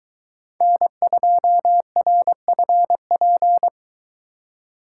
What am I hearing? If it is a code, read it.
Morse code, heard as a keyed tone, about 22 wpm, 700 Hz. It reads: N2RFP